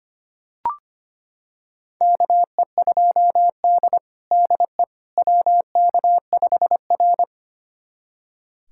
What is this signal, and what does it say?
Morse code, heard as a keyed tone, about 25 wpm, 700 Hz. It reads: KE2D DE WK5R
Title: USA Callsign DE Callsign - 4 Characters - 25wpm